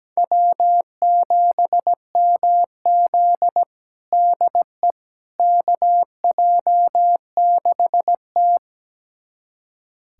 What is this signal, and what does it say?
Morse code, heard as a keyed tone, about 17 wpm, 700 Hz. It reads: W7MZ DE KJ6T
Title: USA Callsign DE Callsign - 4 Characters - 17wpm